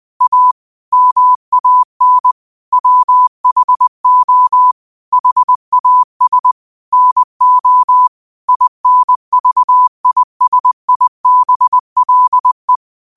Morse code, heard as a keyed tone, about 20 wpm, 1000 Hz. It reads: A MAN WHO HAS NO INVISIBLE